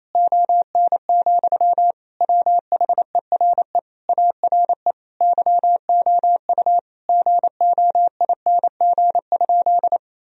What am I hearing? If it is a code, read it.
Morse code, heard as a keyed tone, about 27 wpm, 700 Hz. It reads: ON, WHERE ARE YOU GOING?
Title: Sentences from Top 1000 Words - Review Part 2 - 28wpm